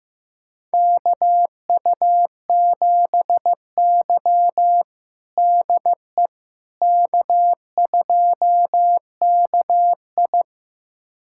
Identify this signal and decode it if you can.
Morse code, heard as a keyed tone, about 16 wpm, 700 Hz. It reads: KU7Y DE K2KI
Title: USA Callsign DE Callsign - 4 Characters - 15wpm